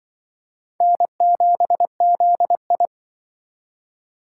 Morse code, heard as a keyed tone, about 24 wpm, 700 Hz. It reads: N7ZI
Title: Speed Racing - USA 1x2 Call Signs 20wpm